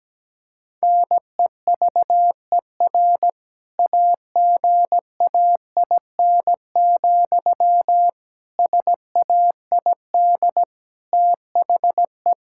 Morse code, heard as a keyed tone, about 17 wpm, 700 Hz. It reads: NEVER AGAIN, SAID THE
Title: Sentences from Top 1600 Words 17wpm